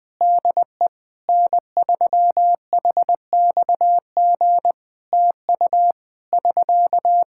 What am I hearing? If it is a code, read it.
Morse code, heard as a keyed tone, about 20 wpm, 700 Hz. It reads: DE N3HXG TU <SK>
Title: Practice QSO - Exchange 4 - Ending - 20wpm